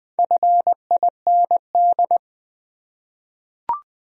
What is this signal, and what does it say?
Morse code, heard as a keyed tone, about 20 wpm, 700 Hz. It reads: FIND
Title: Top 100 Words 20wpm